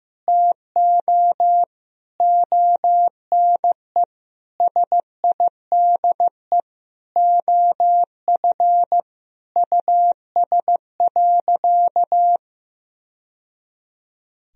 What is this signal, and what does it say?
Morse code, heard as a keyed tone, about 15 wpm, 700 Hz. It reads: TO ONE SIDE OF US.
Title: Sentences from Top 1000 Words - Review Part 3 - 15wpm